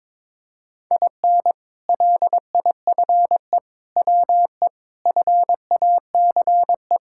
Morse code, heard as a keyed tone, about 22 wpm, 700 Hz. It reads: IN LIFE WE FACE
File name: Sentences with 7-Word Phrases - 22wpm